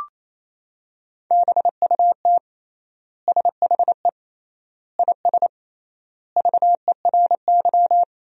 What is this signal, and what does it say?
Morse code, heard as a keyed tone, about 28 wpm, 700 Hz. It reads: BUT SHE IS VERY
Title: 3x Word-Spacing - Sentences from Top 400 Words - 28wpm